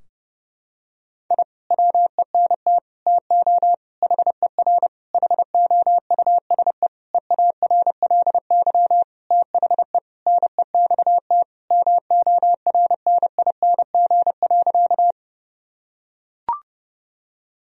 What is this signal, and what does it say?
Morse code, heard as a keyed tone, about 32 wpm, 700 Hz. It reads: I WENT TO HER HOUSE EARLY THE NEXT MORNING.
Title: Sentences from Top 1000 Words - Review Part 2 - 30wpm